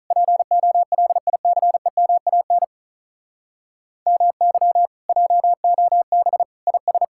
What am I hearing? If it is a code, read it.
Morse code, heard as a keyed tone, about 46 wpm, 700 Hz. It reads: POLICEMAN MY JOB IS